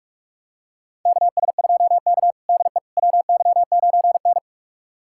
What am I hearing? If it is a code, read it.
Morse code, heard as a keyed tone, about 41 wpm, 700 Hz. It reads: KS2K DE WY9N
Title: USA Callsign DE Callsign - 4 Characters - 45wpm